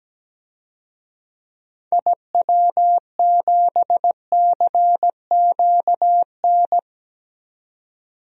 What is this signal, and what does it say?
Morse code, heard as a keyed tone, about 17 wpm, 700 Hz. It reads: IW7CQN
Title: DXCC-100 Call Signs - 6 Characters - 17wpm